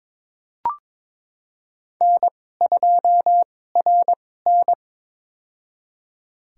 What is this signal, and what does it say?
Morse code, heard as a keyed tone, about 22 wpm, spaced as for 16 wpm, 700 Hz. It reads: N2RN
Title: USA Call Signs - 2x Character Spacing - 22wpm